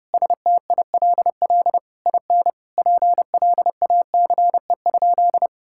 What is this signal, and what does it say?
Morse code, heard as a keyed tone, about 30 wpm, 700 Hz. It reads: STILL IN PLACE?
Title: Sentences from Top 1000 Words - Review Part 6 - 30wpm